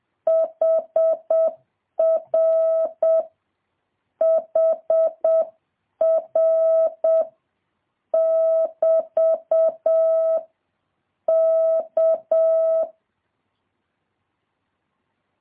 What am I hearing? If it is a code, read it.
Morse code, heard as a keyed tone, about 7 wpm, 650 Hz. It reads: HR HR <BT> K
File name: jh8cbh / 2023cw